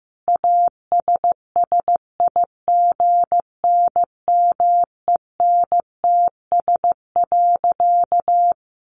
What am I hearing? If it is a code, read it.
Morse code, heard as a keyed tone, about 15 wpm, 700 Hz. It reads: ASSIGNMENTS.
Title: Easy Phrases 15wpm